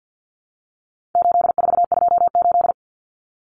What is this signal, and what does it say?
Morse code, heard as a keyed tone, about 50 wpm, 700 Hz. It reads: 8428
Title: Numbers - 4-Digit 50wpm